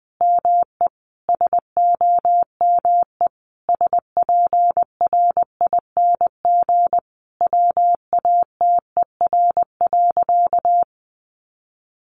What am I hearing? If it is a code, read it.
Morse code, heard as a keyed tone, about 20 wpm, 700 Hz. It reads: ME SOME SPRING WATER.
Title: Sentences from Top 1200 Words 20wpm